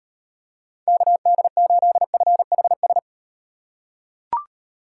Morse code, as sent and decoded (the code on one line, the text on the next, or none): -.- -.. ---.. ..-. .... ...
KD8FHS